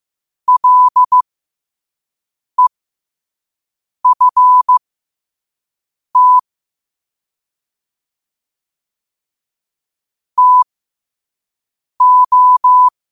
Morse code, heard as a keyed tone, about 15 wpm, spaced as for 5 wpm, 1000 Hz. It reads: LEFT TO